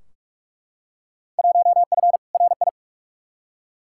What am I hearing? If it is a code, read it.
Morse code, heard as a keyed tone, about 41 wpm, 700 Hz. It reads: 1F RI
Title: Field Day Exchanges Encore - 45wpm